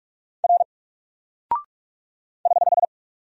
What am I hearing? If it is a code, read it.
Morse code, heard as a keyed tone, about 45 wpm, 700 Hz. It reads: R <HH>
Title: Single Character (Letter:Number:Punctuation:Prosign) Alt 45wpm